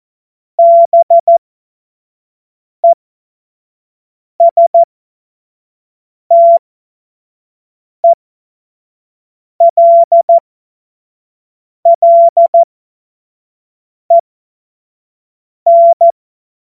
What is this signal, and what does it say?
Morse code, heard as a keyed tone, about 14 wpm, spaced as for 5 wpm, 680 Hz. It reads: BESTELLEN
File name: 25 F70